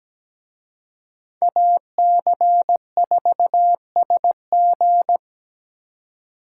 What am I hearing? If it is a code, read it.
Morse code, heard as a keyed tone, about 20 wpm, 700 Hz. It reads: AC4SG
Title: Speed Racing - USA 2x2 Call Signs 17wpm